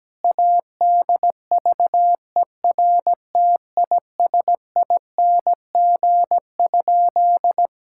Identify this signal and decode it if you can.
Morse code, heard as a keyed tone, about 17 wpm, 700 Hz. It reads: ADVERTISING?
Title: Sentences from Top 1000 Words - Review Part 6 - 17wpm